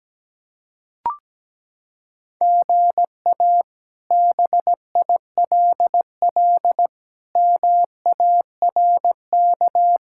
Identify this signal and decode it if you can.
Morse code, heard as a keyed tone, about 17 wpm, 700 Hz. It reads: GA BILL MARK
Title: K1USN SST QSO Practice - 17wpm